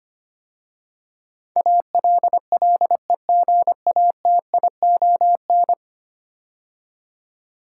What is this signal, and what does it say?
Morse code, heard as a keyed tone, about 25 wpm, 700 Hz. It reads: ALLEGATION
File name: 10 Letter Words - First 3 Letters Given - 25wpm